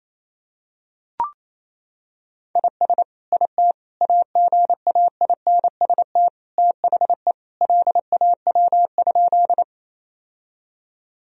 Morse code, heard as a keyed tone, about 28 wpm, 700 Hz. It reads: IS IT AGAINST THE LAW?
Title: Sentences from Top 1000 Words - Review Part 3 - 28wpm